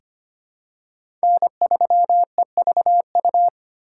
Morse code, heard as a keyed tone, about 25 wpm, 700 Hz. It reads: N3EVU
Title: USA Call Signs - 4x Character Spacing - 25wpm